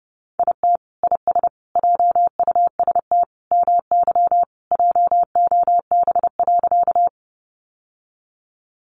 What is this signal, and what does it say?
Morse code, heard as a keyed tone, about 30 wpm, 700 Hz. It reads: IT IS JUST MY JOB.
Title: Sentences from Top 600 Words 30wpm